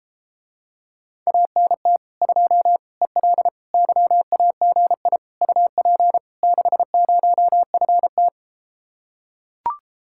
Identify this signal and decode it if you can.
Morse code, heard as a keyed tone, about 34 wpm, 700 Hz. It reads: ANT 2 EL YAGI UP 60FT